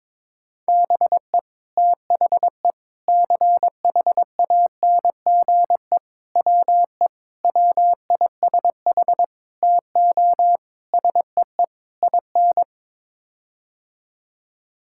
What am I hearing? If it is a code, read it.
Morse code, heard as a keyed tone, about 22 wpm, 700 Hz. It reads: BE THE CHANGE WE WISH TO SEE IN
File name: practice / 22wpm / Sets of 8 Words - 22wpm